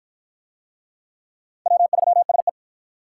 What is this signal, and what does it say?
Morse code, heard as a keyed tone, about 49 wpm, 700 Hz. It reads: W3SE